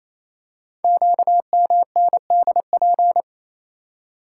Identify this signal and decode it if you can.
Morse code, heard as a keyed tone, about 28 wpm, 700 Hz. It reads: QMNDP
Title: Code Groups - 5 Letters - 28wpm